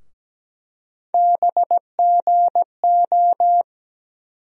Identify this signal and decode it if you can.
Morse code, heard as a keyed tone, about 17 wpm, 700 Hz. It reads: BGO
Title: USA Call Sign Postfixes - 3 Letters 17wpm